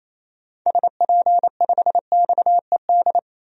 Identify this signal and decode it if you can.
Morse code, heard as a keyed tone, about 28 wpm, 700 Hz. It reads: SP5XED
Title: DXCC-100 Call Signs - 6 Characters - 28wpm